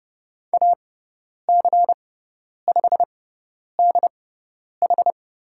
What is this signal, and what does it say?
Morse code, heard as a keyed tone, about 30 wpm, spaced as for 10 wpm, 700 Hz. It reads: AC5DH
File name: Lesson 16g - Letter c - Call Signs 30wpm